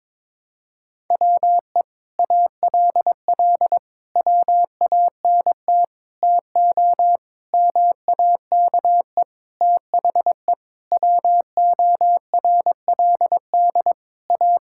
Morse code, heard as a keyed tone, about 22 wpm, 700 Hz. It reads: WE ALL WANT TO MAKE THE WORLD A
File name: Sentences with 7-Word Phrases - 22wpm